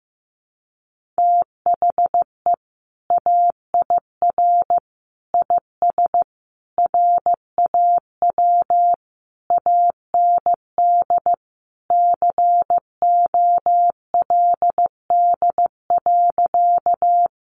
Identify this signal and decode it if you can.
Morse code, heard as a keyed tone, about 15 wpm, 700 Hz. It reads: THE AIR IS RAW AND COLD.
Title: Sentences from Top 1500 Words 15wpm